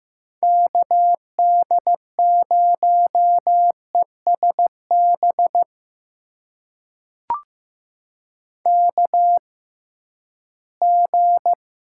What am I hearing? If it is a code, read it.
Morse code, heard as a keyed tone, about 15 wpm, spaced as for 11 wpm, 700 Hz. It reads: KD0ESB KG
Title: USA Call Signs - 6 Characters - 6x Character Spacing - 15wpm